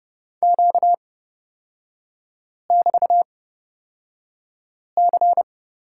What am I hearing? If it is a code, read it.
Morse code, heard as a keyed tone, about 31 wpm, spaced as for 5 wpm, 700 Hz. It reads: Q<BT>C
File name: Lesson 38e - Letter q - Three Characters 30wpm